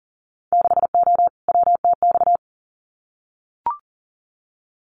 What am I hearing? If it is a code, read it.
Morse code, heard as a keyed tone, about 42 wpm, 700 Hz. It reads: 6O WTX
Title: Winter Field Day Exchanges Encore 40wpm